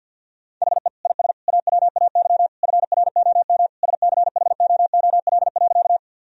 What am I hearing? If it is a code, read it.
Morse code, heard as a keyed tone, about 50 wpm, 700 Hz. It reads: HE IS AWAY FROM SCHOOL.
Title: Sentences from Top 1000 Words - Review Part 1 - 50wpm